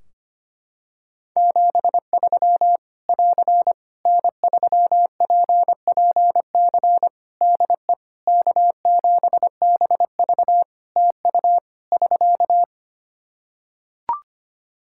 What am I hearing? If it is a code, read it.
Morse code, heard as a keyed tone, about 25 wpm, 700 Hz. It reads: 73 <AR> N3PPC DE K7BV TU <SK>
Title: Practice QSO - Exchange 4 - Ending - 25wpm